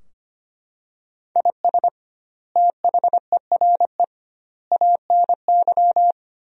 Morse code, heard as a keyed tone, about 25 wpm, 700 Hz. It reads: IS THERE ANY
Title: 2x Word-Spacing - Sentences from Top 300 Words - 25wpm